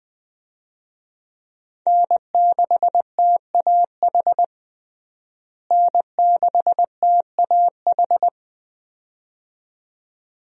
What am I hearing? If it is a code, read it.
Morse code, heard as a keyed tone, about 20 wpm, 700 Hz. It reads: N6TAH N6TAH
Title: USA Call Signs - 5 Characters - Repeated 2x - 20wpm